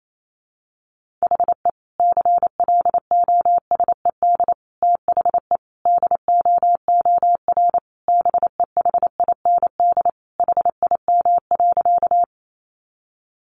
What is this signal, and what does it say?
Morse code, heard as a keyed tone, about 28 wpm, 700 Hz. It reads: HE CLOSED THE DOOR BEHIND HIM.
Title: Sentences from Top 1400 Words 28wpm